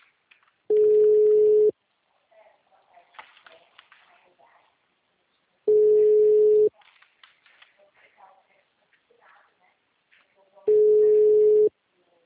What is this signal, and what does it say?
Ringback tone (Europe, 425 Hz), with speech.